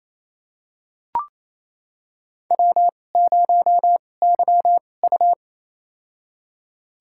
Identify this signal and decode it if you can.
Morse code, heard as a keyed tone, about 28 wpm, spaced as for 20 wpm, 700 Hz. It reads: W0YU
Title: USA Call Signs - 4 Characters - 2x Character Spacing - 28wpm